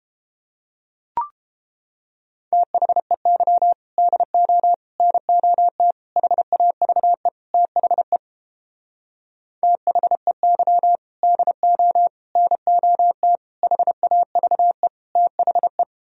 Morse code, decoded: THEY DO NOT HAVE THE THEY DO NOT HAVE THE

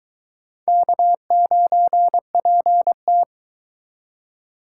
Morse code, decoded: K9PT

